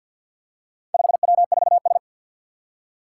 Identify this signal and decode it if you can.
Morse code, heard as a keyed tone, about 53 wpm, 700 Hz. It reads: 5W4S
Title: Four Challenging Letter-Number Combinations 50wpm